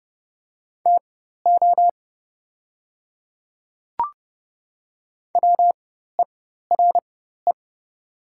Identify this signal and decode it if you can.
Morse code, heard as a keyed tone, about 30 wpm, spaced as for 14 wpm, 700 Hz. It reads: TO WERE
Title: 4x Character Spacing - Top 200 Words - 30wpm